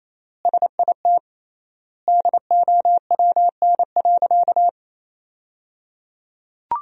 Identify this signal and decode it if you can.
Morse code, heard as a keyed tone, about 28 wpm, 700 Hz. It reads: SIT DOWN.
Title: 3x Word-Spacing - Sentences from Top 500 Words - 28wpm